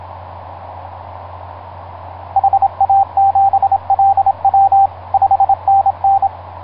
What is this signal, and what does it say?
Morse code, heard as a keyed tone, about 26 wpm, 780 Hz, with radio noise. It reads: HA7LW 5NN
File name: vk9x-g6ay ha7lw etc 10m cw 23feb2010 0836